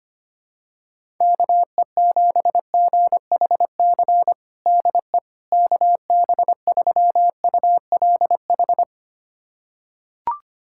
Morse code, heard as a keyed tone, about 25 wpm, 700 Hz. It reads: KE7GHC DE KB3ULH
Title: USA Callsign DE Callsign - 6 Characters - 25wpm